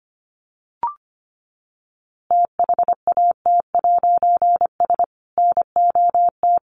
Morse code, heard as a keyed tone, about 25 wpm, 700 Hz. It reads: THAT'S NOT
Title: A Sound Of Thunder 25wpm